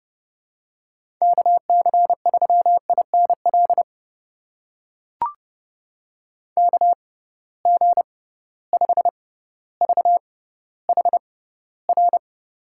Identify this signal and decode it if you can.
Morse code, heard as a keyed tone, about 30 wpm, spaced as for 15 wpm, 700 Hz. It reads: KC3INL KG5VHR
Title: USA Call Signs - 6 Characters - 6x Character Spacing - 30wpm